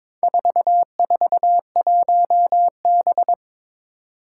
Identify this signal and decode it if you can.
Morse code, heard as a keyed tone, about 22 wpm, 700 Hz. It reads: 441B
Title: Four Challenging Letter-Number Combinations 22wpm